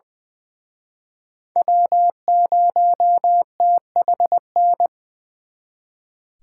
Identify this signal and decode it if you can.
Morse code, heard as a keyed tone, about 23 wpm, 700 Hz. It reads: W0THN
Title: Speed Racing - USA 1x3 Call Signs 20wpm